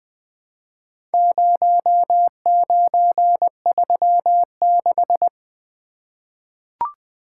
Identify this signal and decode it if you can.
Morse code, heard as a keyed tone, about 20 wpm, 700 Hz. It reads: 0936
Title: Four Letter-Number Combinations - 20wpm